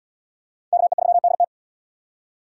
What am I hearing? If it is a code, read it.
Morse code, heard as a keyed tone, about 72 wpm, 700 Hz. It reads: KE3DT